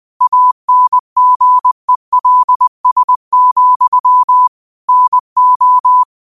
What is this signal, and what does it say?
Morse code, heard as a keyed tone, about 20 wpm, 1000 Hz. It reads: ANGELS, NO
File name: qotdfiles / 20wpm